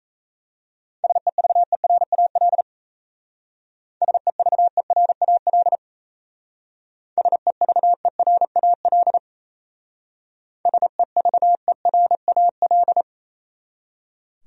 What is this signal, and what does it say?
Morse code, heard as a keyed tone, about 34 wpm, 700 Hz. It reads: SEVERAL SEVERAL SEVERAL SEVERAL
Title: Speed-Racing Top 500 Words 28wpm